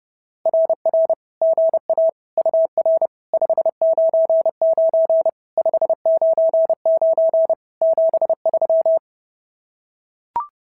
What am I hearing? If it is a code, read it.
Morse code, heard as a keyed tone, about 30 wpm, 590 Hz. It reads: RR GA UR 599 599 73
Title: SOTA QSO Practice 30wpm